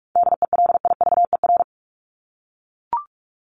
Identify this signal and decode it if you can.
Morse code, heard as a keyed tone, about 44 wpm, 700 Hz. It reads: DELIVER
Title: Top 1751-2000 Words 45wpm